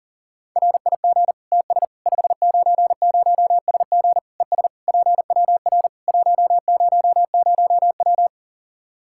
Morse code, heard as a keyed tone, about 40 wpm, 700 Hz. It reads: RIG TS 590SG ES PWR 100W
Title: Practice QSO - Exchange 2 - Info - 40wpm